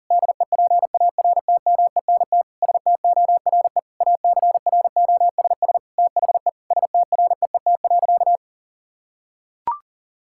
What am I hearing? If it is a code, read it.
Morse code, heard as a keyed tone, about 40 wpm, 700 Hz. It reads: DEPARTMENT STORE ACROSS THE STREET.
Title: Sentences from Top 1000 Words - Review Part 5 - 40wpm